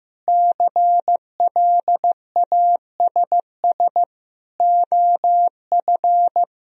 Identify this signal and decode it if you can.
Morse code, heard as a keyed tone, about 15 wpm, 700 Hz. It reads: CLASS OF
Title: Sentences from Top 1000 Words - Review Part 2 - 15wpm